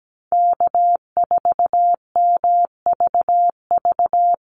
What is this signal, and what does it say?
Morse code, heard as a keyed tone, about 17 wpm, 700 Hz. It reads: K4MVV
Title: USA Call Signs Repeated 3x 17wpm